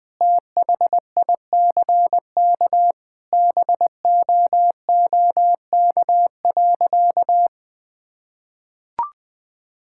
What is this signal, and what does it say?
Morse code, heard as a keyed tone, about 20 wpm, 700 Hz. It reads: THICK BOOK.
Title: Sentences with 4-Word Phrases - 20wpm